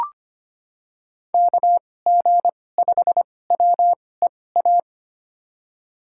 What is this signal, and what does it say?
Morse code, heard as a keyed tone, about 25 wpm, spaced as for 18 wpm, 700 Hz. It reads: KG5WEA
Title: USA Call Signs - 6 Characters - 2x Character Spacing - 25wpm